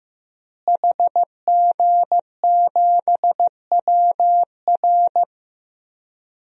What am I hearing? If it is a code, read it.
Morse code, heard as a keyed tone, about 15 wpm, 700 Hz. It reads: HG7WR
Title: DXCC-100 Call Signs - 5 Characters - Repeated 2x - 15wpm